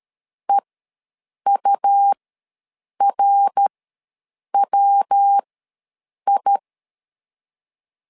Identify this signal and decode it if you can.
Morse code, heard as a keyed tone, about 13 wpm, spaced as for 7 wpm, 800 Hz. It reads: EURWI
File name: LEZIONI / 3 MWRO / 3 MWR ws 12wpm 1809